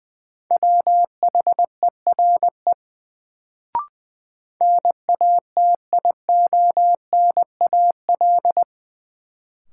Morse code, heard as a keyed tone, about 20 wpm, 700 Hz. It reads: WHERE NATIONAL
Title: Sets of 2 Words - New Word Drill - 20wpm